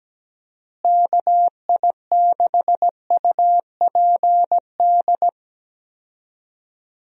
Morse code, heard as a keyed tone, about 17 wpm, 700 Hz. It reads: KI6UPD
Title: USA Call Signs - 6 Characters - Repeated 3x - 17wpm